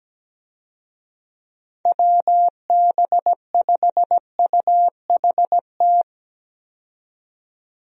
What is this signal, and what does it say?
Morse code, heard as a keyed tone, about 17 wpm, 700 Hz. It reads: WB5UHT
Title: USA Call Signs - 6 Characters - 17wpm